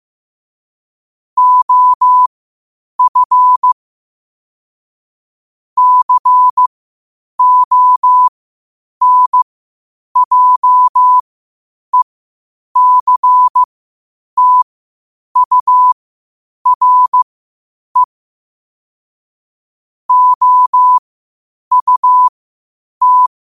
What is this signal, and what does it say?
Morse code, heard as a keyed tone, about 15 wpm, spaced as for 8 wpm, 1000 Hz. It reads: OF CONJECTURE OUT
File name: qotdfiles / 8wpm